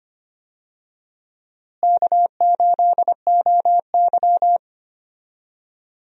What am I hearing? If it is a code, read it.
Morse code, heard as a keyed tone, about 25 wpm, 700 Hz. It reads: K8OY
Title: USA Call Signs - 4 Characters - Repeated 3x - 25wpm